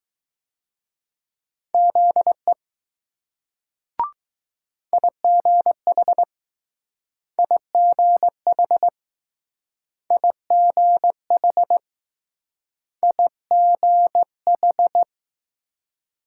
Morse code, heard as a keyed tone, about 20 wpm, 700 Hz. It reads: ZE IGH IGH IGH IGH